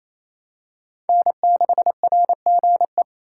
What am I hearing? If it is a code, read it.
Morse code, heard as a keyed tone, about 28 wpm, 700 Hz. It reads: N6RGE
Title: USA Call Signs - 5 Characters - 4x Character Spacing - 28wpm